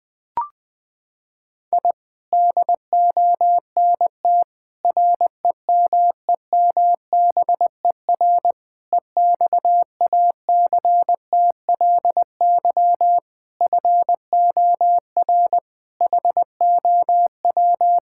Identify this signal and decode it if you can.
Morse code, heard as a keyed tone, about 20 wpm, 700 Hz. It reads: I DONT REMEMBER EXACTLY FOR HOW
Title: Sentences from Top 1000 Words - Review Part 6 - 20wpm